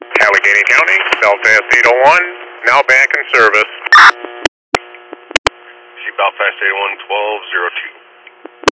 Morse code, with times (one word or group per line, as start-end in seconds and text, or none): none